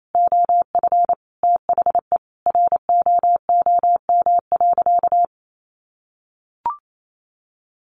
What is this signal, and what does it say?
Morse code, heard as a keyed tone, about 28 wpm, 700 Hz. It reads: OF THE ROOM.
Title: Sentences from Top 500 Words 28wpm